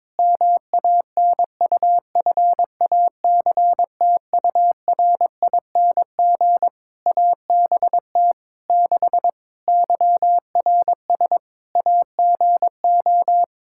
Morse code, heard as a keyed tone, about 22 wpm, 700 Hz. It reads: MANUFACTURING ABT 6 YRS AGO
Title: Practice QSO - Exchange 3 - More Info - 22wpm